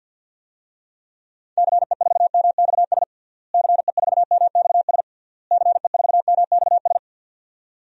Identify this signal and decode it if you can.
Morse code, heard as a keyed tone, about 50 wpm, 700 Hz. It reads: CE4MXS CE4MXS CE4MXS
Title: DXCC-100 Call Signs - 6 Characters - Repeated 3x - 50wpm